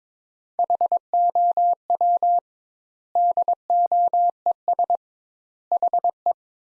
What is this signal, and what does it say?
Morse code, heard as a keyed tone, about 22 wpm, 700 Hz. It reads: HOW DOES HE
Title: Sentences Top 200 Words - 22wpm